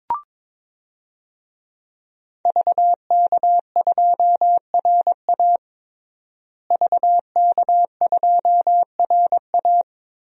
Morse code, heard as a keyed tone, about 22 wpm, 700 Hz. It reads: VK2RA VK2RA
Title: DXCC-100 Call Signs - 5 Characters - Repeated 3x - 22wpm